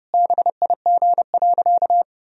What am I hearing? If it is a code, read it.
Morse code, heard as a keyed tone, about 32 wpm, 700 Hz. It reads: BIG.
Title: Sentences from Top 1000 Words - Review Part 1 - 30wpm